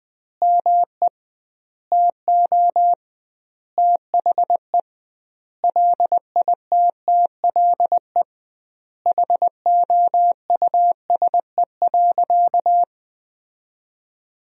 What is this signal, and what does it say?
Morse code, heard as a keyed tone, about 20 wpm, 700 Hz. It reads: ME TO THE LITTLE HOUSE.
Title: 2x Word-Spacing - Sentences from Top 200 Words - 20wpm